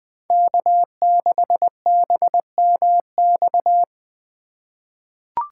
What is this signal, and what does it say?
Morse code, heard as a keyed tone, about 20 wpm, 700 Hz. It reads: K6BMX
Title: USA Call Signs - 5 Characters - 4x Character Spacing - 20wpm